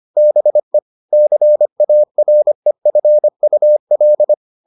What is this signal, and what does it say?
Morse code, heard as a keyed tone, about 25 wpm, 600 Hz. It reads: BE CAREFUL